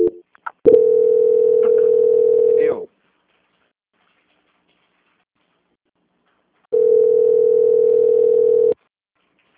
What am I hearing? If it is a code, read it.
Ringback tone (North America), 440 + 480 Hz.
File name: out- -2003- - - .29703